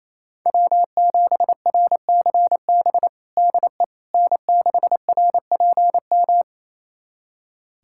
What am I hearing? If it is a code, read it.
Morse code, heard as a keyed tone, about 28 wpm, 700 Hz. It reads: W7RCB DE N6RPM
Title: USA Callsign DE Callsign - 5 Characters - 28wpm